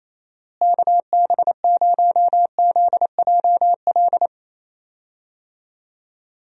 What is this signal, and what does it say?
Morse code, heard as a keyed tone, about 28 wpm, 700 Hz. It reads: KB0ZJL
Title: USA Call Signs - 6 Characters - Repeated 2x - 28wpm